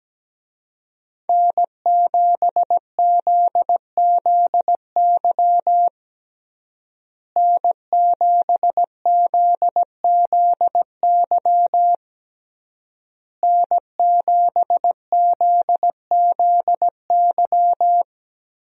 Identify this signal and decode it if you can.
Morse code, heard as a keyed tone, about 17 wpm, 700 Hz. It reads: N7ZZY N7ZZY N7ZZY
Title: USA Call Signs - 5 Characters - Repeated 3x - 17wpm